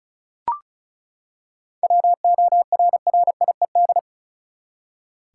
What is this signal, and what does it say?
Morse code, heard as a keyed tone, about 34 wpm, 700 Hz. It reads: WORRIED
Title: Seven Letter Words - 35wpm